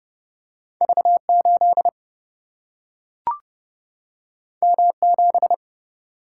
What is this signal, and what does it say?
Morse code, heard as a keyed tone, about 30 wpm, 700 Hz. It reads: V 8 M 7
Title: Single-Letter Single-Number 30wpm